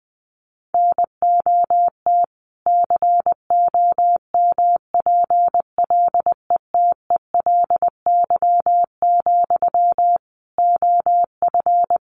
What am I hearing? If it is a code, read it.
Morse code, heard as a keyed tone, about 20 wpm, 700 Hz. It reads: NOT COMPLETELY, OF
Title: Sentences from Top 1100 Words 20wpm